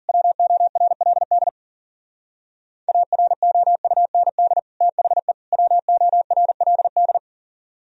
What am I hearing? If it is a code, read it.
Morse code, heard as a keyed tone, about 50 wpm, 700 Hz. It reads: WORLD AROUND THE WORLD
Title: Speed-Racing - Sets of 3 Words 40wpm